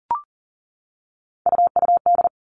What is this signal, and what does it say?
Morse code, heard as a keyed tone, about 40 wpm, 700 Hz. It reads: UUD
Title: All Cut Numbers - 3-Digit 40wpm